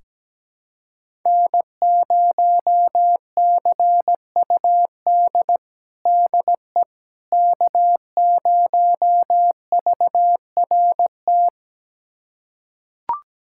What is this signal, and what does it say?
Morse code, heard as a keyed tone, about 17 wpm, 700 Hz. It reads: N0CUD DE K0VRT
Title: USA Callsign DE Callsign - 5 Characters - 17wpm